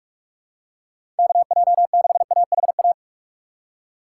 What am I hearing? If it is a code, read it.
Morse code, heard as a keyed tone, about 46 wpm, 700 Hz. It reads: KJ6AHA
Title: USA Call Signs - 6 Characters - 3x Character Spacing - 45wpm